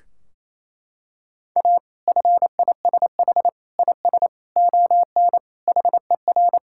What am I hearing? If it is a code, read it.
Morse code, heard as a keyed tone, about 28 wpm, 700 Hz. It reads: A FISH IS ON HER
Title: Sentences from Top 1600 Words 28wpm